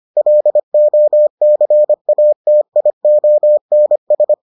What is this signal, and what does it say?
Morse code, heard as a keyed tone, about 25 wpm, 600 Hz. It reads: LOCATIONS